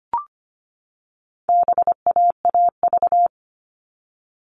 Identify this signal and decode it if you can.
Morse code, heard as a keyed tone, about 25 wpm, 700 Hz. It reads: BAAV
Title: All Cut Numbers - 4-Digit 25wpm